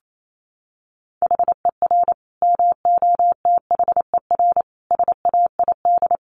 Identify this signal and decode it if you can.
Morse code, heard as a keyed tone, about 28 wpm, 700 Hz. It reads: HER MOTHER SAID
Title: Sentences from Top 1200 Words 28wpm